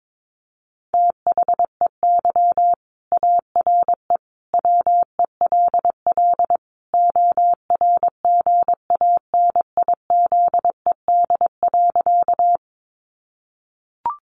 Morse code, heard as a keyed tone, about 22 wpm, 700 Hz. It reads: THEY ARE WELL ORGANIZED.
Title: Sentences from Top 1300 Words 22wpm